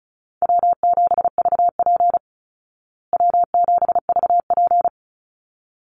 Morse code, heard as a keyed tone, about 34 wpm, 700 Hz. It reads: W7VP W7VP
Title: Call Signs Repeated 3x - 35wpm